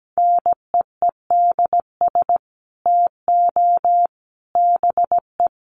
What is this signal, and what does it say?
Morse code, heard as a keyed tone, about 17 wpm, 700 Hz. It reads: NEEDS TO BE
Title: Sets of 3 Words 17wpm